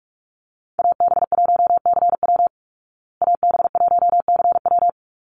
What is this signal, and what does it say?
Morse code, heard as a keyed tone, about 44 wpm, 700 Hz. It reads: AB1CW AB1CW
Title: Call Signs Repeated 3x - 45wpm